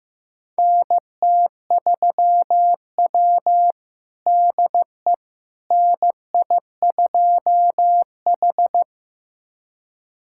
Morse code, heard as a keyed tone, about 15 wpm, 700 Hz. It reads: NT3W DE NI2H
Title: USA Callsign DE Callsign - 4 Characters - 15wpm